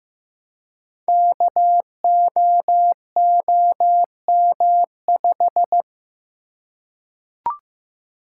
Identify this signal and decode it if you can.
Morse code, heard as a keyed tone, about 15 wpm, 700 Hz. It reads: KOOM5
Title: Code Groups - 5 Letter:Numbers - 15wpm